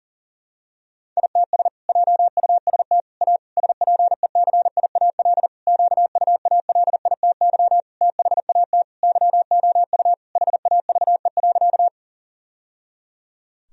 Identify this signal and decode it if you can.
Morse code, heard as a keyed tone, about 40 wpm, 700 Hz. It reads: ITS JUST A SPECIAL QUALITY THAT YOU HAVE.
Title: Sentences from Top 1000 Words - Review Part 4 - 40wpm